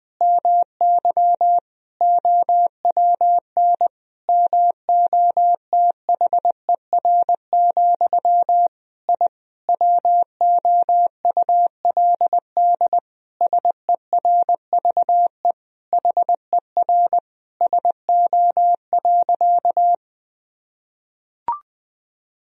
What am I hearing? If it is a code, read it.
Morse code, heard as a keyed tone, about 20 wpm, 700 Hz. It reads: MY OWN MOTHER, I WOULD SERVE HER SO.
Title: Sentences from Top 1000 Words - Review Part 2 - 20wpm